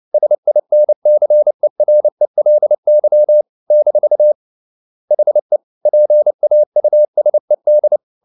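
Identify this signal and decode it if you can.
Morse code, heard as a keyed tone, about 29 wpm, 600 Hz. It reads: SINCERELY - HE PAUSED
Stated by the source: Morse code at 29 wpm, sending Ken Follett's novel Never in English, from CW med Gnister.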